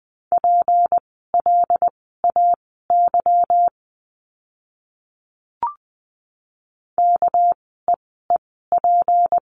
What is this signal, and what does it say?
Morse code, heard as a keyed tone, about 20 wpm, spaced as for 15 wpm, 700 Hz. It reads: PLAY KEEP
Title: 244 Lesson 39f - Letter z - Words 20wpm